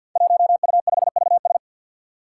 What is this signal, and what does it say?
Morse code, heard as a keyed tone, about 47 wpm, 700 Hz. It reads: 1U5VS
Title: Five Challenging Letter-Number Combinations 50wpm